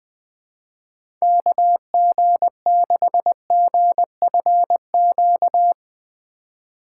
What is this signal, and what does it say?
Morse code, heard as a keyed tone, about 20 wpm, 700 Hz. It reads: KG6GFQ
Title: USA Call Signs - 6 Characters - 4x Character Spacing - 20wpm